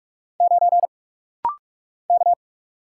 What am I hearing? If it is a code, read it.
Morse code, heard as a keyed tone, about 44 wpm, 700 Hz. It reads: 9 K
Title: Single Letter-Number - Rapid-Fire 45wpm